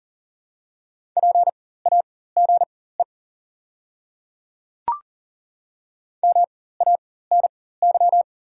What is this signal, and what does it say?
Morse code, heard as a keyed tone, about 37 wpm, spaced as for 18 wpm, 700 Hz. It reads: PAGE MANY
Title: 4x Character Spacing - Top 300 Words - 40wpm